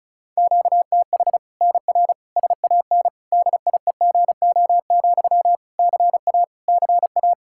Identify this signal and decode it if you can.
Morse code, heard as a keyed tone, about 36 wpm, 700 Hz. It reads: QTH NR SAN DIEGO, CA CA
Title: Practice QSO - Exchange 1 - Beginning - 35wpm